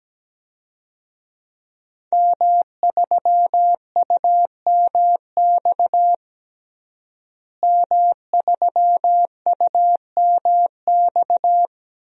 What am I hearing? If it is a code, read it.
Morse code, heard as a keyed tone, about 17 wpm, 700 Hz. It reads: M3UMX M3UMX
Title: DXCC-100 Call Signs - 5 Characters - Repeated 3x - 17wpm